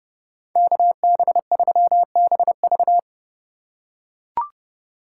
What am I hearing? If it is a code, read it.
Morse code, heard as a keyed tone, about 30 wpm, 700 Hz. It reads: KB3BV
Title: USA Call Signs - 5 Characters - 4x Character Spacing - 30wpm